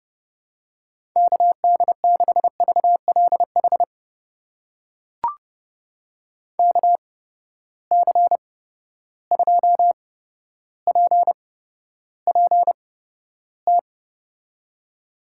Morse code, heard as a keyed tone, about 32 wpm, spaced as for 13 wpm, 700 Hz. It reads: KD6VLH KC2PPT
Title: USA Call Signs - 6 Characters - 8x Character Spacing - 30wpm